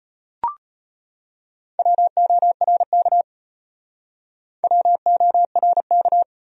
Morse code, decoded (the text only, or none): WORK WORK